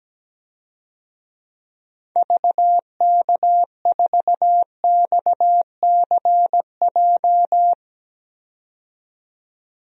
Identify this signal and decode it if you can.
Morse code, heard as a keyed tone, about 17 wpm, 700 Hz. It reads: VK4XCJ
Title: DXCC-100 Call Signs - 6 Characters - 17wpm